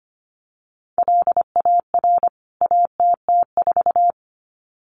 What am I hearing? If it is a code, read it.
Morse code, heard as a keyed tone, about 25 wpm, 700 Hz. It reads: LAR ATT4
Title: CWT Exchange 25wpm